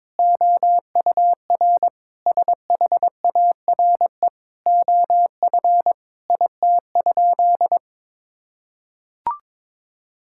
Morse code, heard as a keyed tone, about 22 wpm, 700 Hz. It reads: OUR SHARE OF IT?
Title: Sentences from Top 1000 Words - Review Part 6 - 22wpm